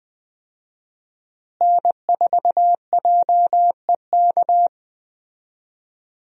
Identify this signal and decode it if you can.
Morse code, heard as a keyed tone, about 20 wpm, 700 Hz. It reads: N4JEK
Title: USA Call Signs - 5 Characters - Repeated 2x - 20wpm